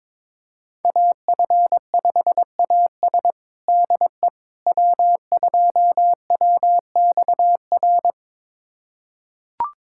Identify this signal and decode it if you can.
Morse code, heard as a keyed tone, about 22 wpm, 700 Hz. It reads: AF5AS DE W2WXR
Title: USA Callsign DE Callsign - 5 Characters - 22wpm